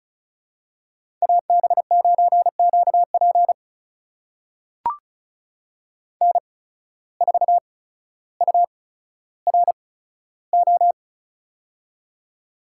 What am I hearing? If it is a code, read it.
Morse code, heard as a keyed tone, about 34 wpm, spaced as for 15 wpm, 700 Hz. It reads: AB9QP N4URO
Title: USA Call Signs - 5 Characters - 8x Character Spacing - 35wpm